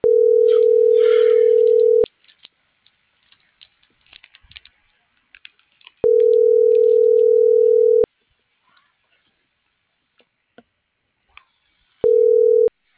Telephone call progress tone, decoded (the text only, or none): ringback tone